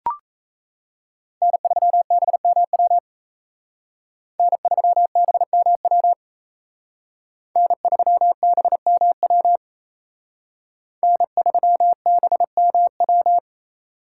Morse code, heard as a keyed tone, about 36 wpm, 700 Hz. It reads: N3BMW N3BMW N3BMW N3BMW